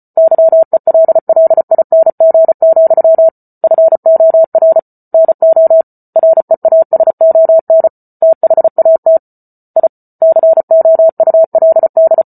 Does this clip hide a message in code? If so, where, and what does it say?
Yes: Morse code from 0.2 to 12.2 s, heard as a keyed tone, about 34 wpm, 650 Hz. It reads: YELLING, FOR NO REASON THAT I COULD